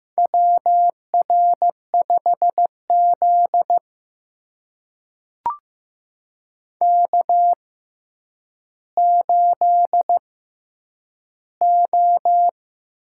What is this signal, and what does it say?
Morse code, heard as a keyed tone, about 15 wpm, spaced as for 8 wpm, 700 Hz. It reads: WR5Z K8O
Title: USA Call Signs - 4 Characters - 6x Character Spacing - 15wpm